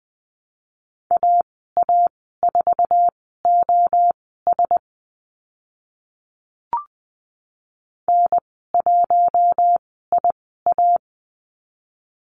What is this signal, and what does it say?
Morse code, heard as a keyed tone, about 20 wpm, spaced as for 14 wpm, 700 Hz. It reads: AA4OS N1IA
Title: 048 Lesson 9g - Number 4 - Call Signs 20wpm